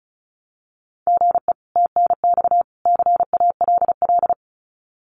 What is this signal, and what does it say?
Morse code, heard as a keyed tone, about 34 wpm, 700 Hz. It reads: GE TNX CALL
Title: Common QSO Phrases 35wpm